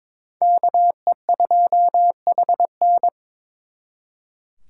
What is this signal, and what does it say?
Morse code, heard as a keyed tone, about 26 wpm, 700 Hz. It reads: KE2HN
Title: Speed Racing - USA Call Signs 22wpm